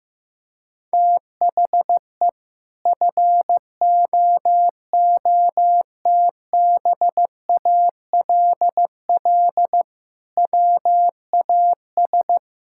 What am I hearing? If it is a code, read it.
Morse code, heard as a keyed tone, about 16 wpm, 700 Hz. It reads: THE FOOTBALL WAS
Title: Sentences from Top 1600 Words 15wpm